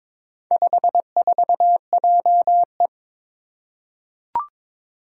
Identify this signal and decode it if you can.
Morse code, heard as a keyed tone, about 22 wpm, 700 Hz. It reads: 54JE
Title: Four Challenging Letter-Number Combinations 22wpm